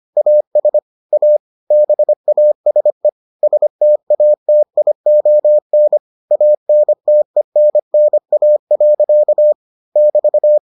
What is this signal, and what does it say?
Morse code, heard as a keyed tone, about 25 wpm, 600 Hz. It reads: AS A BASE STATION ANTENNA. <BT>